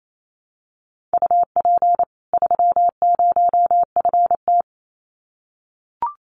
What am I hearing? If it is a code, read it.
Morse code, heard as a keyed tone, about 28 wpm, 700 Hz. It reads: UP 30FT
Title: Common QSO Phrases 28wpm